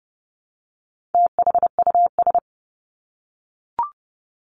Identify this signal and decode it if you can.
Morse code, heard as a keyed tone, about 30 wpm, 700 Hz. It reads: THUS
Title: Top 751-1000 Words 30wpm